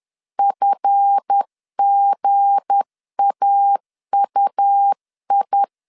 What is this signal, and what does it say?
Morse code, heard as a keyed tone, about 10 wpm, 800 Hz. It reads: FGAUI
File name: LEZIONI / 7 GF9Z / 7 GF 10wpm 2514